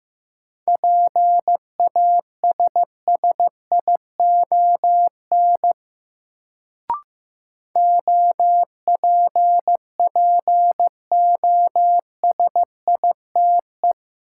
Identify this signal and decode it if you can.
Morse code, heard as a keyed tone, about 15 wpm, 700 Hz. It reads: PASSION OPPOSITE
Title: Sentences from Top 1300 Words - New Word Drill - 15wpm